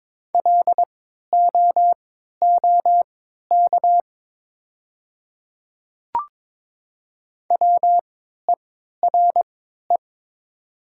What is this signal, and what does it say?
Morse code, heard as a keyed tone, about 21 wpm, spaced as for 12 wpm, 690 Hz. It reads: LOOK WERE
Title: 3x Character Spacing - Top 200 Words - 22wpm